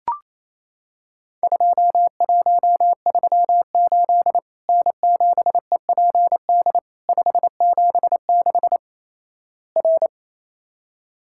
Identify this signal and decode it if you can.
Morse code, heard as a keyed tone, about 28 wpm, 700 Hz. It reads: 2138 N7EPD 576 R
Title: Worked All Europe DX Contest - QTC Exchange - 28wpm